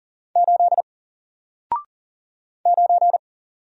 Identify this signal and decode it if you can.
Morse code, heard as a keyed tone, about 40 wpm, 700 Hz. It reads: 8 9
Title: Single Letter-Number Alt 40wpm